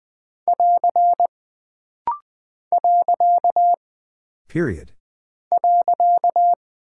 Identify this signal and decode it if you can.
Morse code, heard as a keyed tone, about 20 wpm, 700 Hz. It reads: <AR> . .
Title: Single Punctuation-Prosign - Rapid-Fire - 20wpm